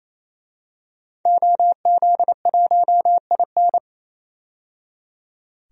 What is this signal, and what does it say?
Morse code, heard as a keyed tone, about 28 wpm, 700 Hz. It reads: OZ1IN
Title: DXCC-100 Call Signs - 5 Characters - 28wpm